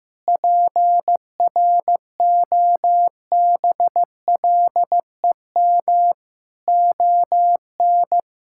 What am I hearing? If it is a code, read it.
Morse code, heard as a keyed tone, about 15 wpm, 700 Hz. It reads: PROBLEM ON
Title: Sentences from Top 1000 Words - Review Part 6 - 15wpm